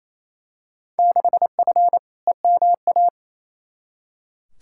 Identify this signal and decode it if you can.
Morse code, heard as a keyed tone, about 28 wpm, 700 Hz. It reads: 6F EMA
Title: Field Day Exchanges Encore - 28wpm